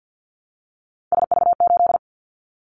Morse code, heard as a keyed tone, about 50 wpm, 700 Hz. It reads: SV8
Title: Callsign Prefixes for Top 50 DXCC - 50wpm